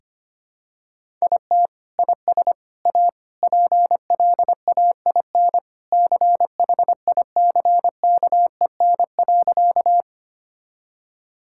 Morse code, heard as a keyed tone, about 25 wpm, 700 Hz. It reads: IT IS A PLAIN CHICKEN.
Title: Sentences from Top 1000 Words - Review Part 5 - 25wpm